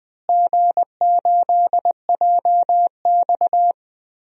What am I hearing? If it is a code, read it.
Morse code, heard as a keyed tone, about 20 wpm, 700 Hz. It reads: G8JX
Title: DXCC-100 Call Signs - 4 Characters - Repeated 2x - 20wpm